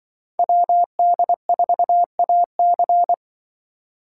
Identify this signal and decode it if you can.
Morse code, heard as a keyed tone, about 25 wpm, 700 Hz. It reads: WD4AC